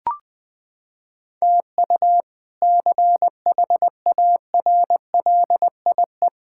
Morse code, heard as a keyed tone, about 20 wpm, 700 Hz. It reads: TU CHARLIE
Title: K1USN SST QSO Practice - 20wpm